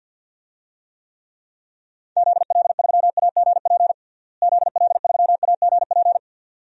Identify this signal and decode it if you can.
Morse code, heard as a keyed tone, about 46 wpm, 700 Hz. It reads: ZL3AGP ZL3AGP
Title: DXCC-100 Call Signs - 6 Characters - Repeated 2x - 50wpm